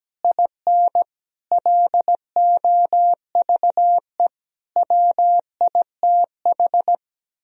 Morse code, heard as a keyed tone, about 17 wpm, 700 Hz. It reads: IN LOVE WITH
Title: Sentences from Top 1000 Words - Review Part 3 - 17wpm